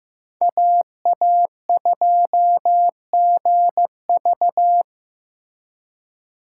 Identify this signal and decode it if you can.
Morse code, heard as a keyed tone, about 16 wpm, 700 Hz. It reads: AA2GV
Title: USA Call Signs - 5 Characters - 2x Character Spacing - 15wpm